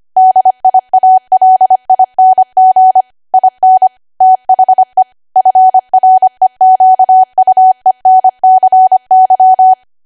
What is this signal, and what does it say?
Morse code, heard as a keyed tone, about 25 wpm, 750 Hz. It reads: DIALING IN THE FREQUENCY
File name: Archive / 25 WPM / 25WPM